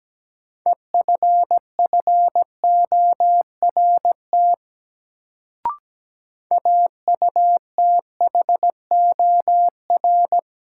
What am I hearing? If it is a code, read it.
Morse code, heard as a keyed tone, about 17 wpm, 700 Hz. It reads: EFFORT AUTHOR
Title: Sentences from Top 1200 Words - New Word Drill - 17wpm